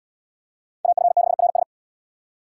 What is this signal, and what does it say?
Morse code, heard as a keyed tone, about 68 wpm, 700 Hz. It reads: AF6RS